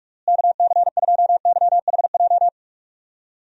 Morse code, heard as a keyed tone, about 45 wpm, 700 Hz. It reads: KC2YHJ
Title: USA Call Signs - 6 Characters - Repeated 3x - 45wpm